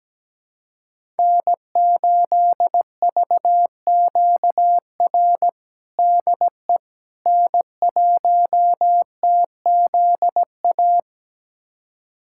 Morse code, heard as a keyed tone, about 17 wpm, 700 Hz. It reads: N8VQR DE N1TZA
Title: USA Callsign DE Callsign - 5 Characters - 17wpm